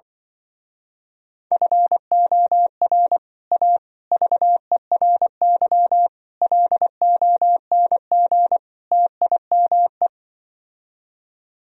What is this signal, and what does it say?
Morse code, heard as a keyed tone, about 25 wpm, 700 Hz. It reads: FOR A VERY LONG TIME
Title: Speed-Racing - Sets of 5 Words 20wpm